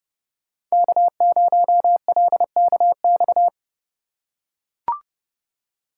Morse code, heard as a keyed tone, about 30 wpm, 700 Hz. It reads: K0LKX
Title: USA Call Signs - 5 Characters - 4x Character Spacing - 30wpm